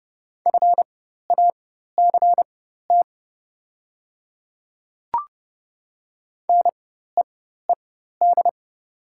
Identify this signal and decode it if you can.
Morse code, heard as a keyed tone, about 30 wpm, spaced as for 14 wpm, 690 Hz. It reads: FACT NEED
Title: 4x Character Spacing - Top 500 Words - 30wpm